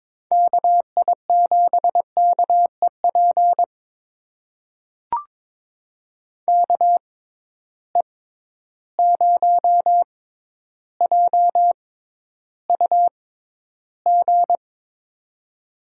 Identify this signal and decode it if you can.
Morse code, heard as a keyed tone, about 22 wpm, spaced as for 11 wpm, 700 Hz. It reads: KI7KEP KE0JUG
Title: USA Call Signs - 6 Characters - 6x Character Spacing - 22wpm